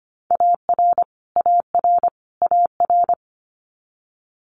Morse code, heard as a keyed tone, about 25 wpm, 700 Hz. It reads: AR AR AR
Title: Multi-letter Phonemes Repeated 3x 25wpm